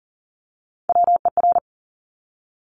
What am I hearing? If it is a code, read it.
Morse code, heard as a keyed tone, about 42 wpm, 700 Hz. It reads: WER
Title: Top Three Letter Combinations in Words 40wpm